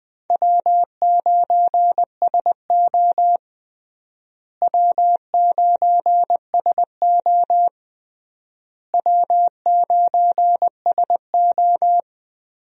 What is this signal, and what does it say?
Morse code, heard as a keyed tone, about 21 wpm, 700 Hz. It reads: W9SO W9SO W9SO
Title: USA Call Signs - 4 Characters - Repeated 3x - 20wpm